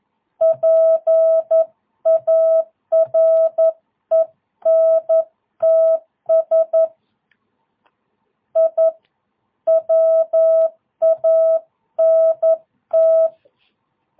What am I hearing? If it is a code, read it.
Morse code, heard as a keyed tone, about 11 wpm, 650 Hz. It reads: PARENTS I WANT
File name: jh8cbh / 2022cw